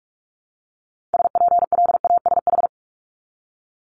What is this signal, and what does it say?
Morse code, heard as a keyed tone, about 46 wpm, 700 Hz. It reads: SPLASH
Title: Multi-letter Phonemes Repeated 3x 45wpm